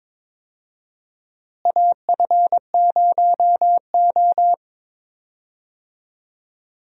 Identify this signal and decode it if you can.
Morse code, heard as a keyed tone, about 22 wpm, 700 Hz. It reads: AF0O
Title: USA Call Signs - 4 Characters - Repeated 3x - 22wpm